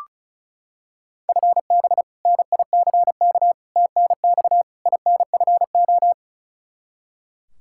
Morse code, heard as a keyed tone, about 36 wpm, 700 Hz. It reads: FB NICK TNX INFO
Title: Practice QSO - Exchange 3 - More Info - 35wpm